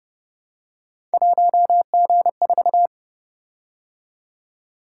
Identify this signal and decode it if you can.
Morse code, heard as a keyed tone, about 29 wpm, 700 Hz. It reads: 1G4
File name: Speed-Racing - Three Letter-Number Combinations - 25wpm